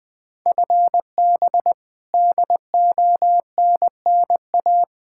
Morse code, heard as a keyed tone, about 20 wpm, 700 Hz. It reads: FB DONNA
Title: Practice QSO - Exchange 3 - More Info - 20wpm